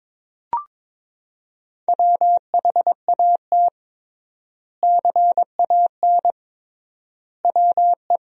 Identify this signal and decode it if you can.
Morse code, heard as a keyed tone, about 22 wpm, 700 Hz. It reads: WHAT CAN WE
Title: 3x Word-Spacing - Sentences from Top 100 Words - 22wpm